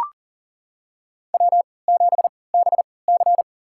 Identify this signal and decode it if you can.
Morse code, heard as a keyed tone, about 40 wpm, spaced as for 23 wpm, 700 Hz. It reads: W7BC
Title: USA Call Signs - 4 Characters - 3x Character Spacing - 40wpm